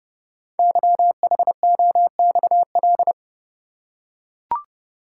Morse code, heard as a keyed tone, about 30 wpm, 700 Hz. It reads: YHOXL
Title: Code Groups - 5 Letters - 30wpm